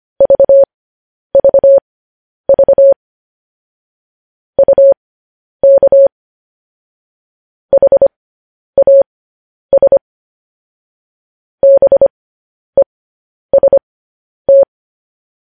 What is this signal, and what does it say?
Morse code, heard as a keyed tone, about 25 wpm, spaced as for 10 wpm, 550 Hz. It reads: VVV UK HAS BEST
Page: News Headlines in Morse Code at 10 WPM